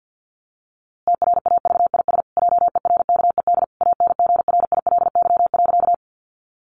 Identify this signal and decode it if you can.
Morse code, heard as a keyed tone, about 50 wpm, 700 Hz. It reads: TRAVIS JERKED ANGRILY.
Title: A Sound Of Thunder 50wpm